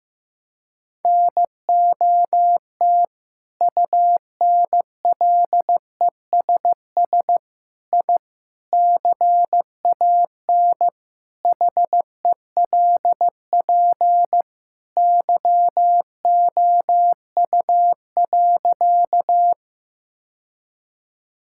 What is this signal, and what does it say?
Morse code, heard as a keyed tone, about 15 wpm, 700 Hz. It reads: NOT UNLESS I CAN HELP YOU.
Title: Sentences from Top 1000 Words - Review Part 6 - 15wpm